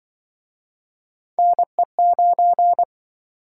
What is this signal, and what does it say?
Morse code, heard as a keyed tone, about 28 wpm, 700 Hz. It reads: NE9